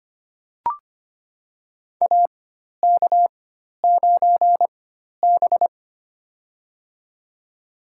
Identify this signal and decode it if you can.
Morse code, heard as a keyed tone, about 25 wpm, spaced as for 12 wpm, 700 Hz. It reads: AK9B
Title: USA Call Signs - 4 Characters - 4x Character Spacing - 25wpm